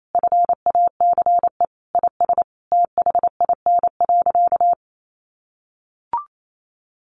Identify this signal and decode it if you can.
Morse code, heard as a keyed tone, about 28 wpm, 700 Hz. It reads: FACE IS THIN.
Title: Sentences from Top 1200 Words 28wpm